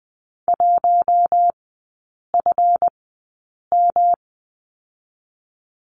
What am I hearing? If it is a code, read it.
Morse code, heard as a keyed tone, about 20 wpm, spaced as for 8 wpm, 690 Hz. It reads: 1FM
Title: 212 Lesson 34e - Number 8 - Three Characters 20wpm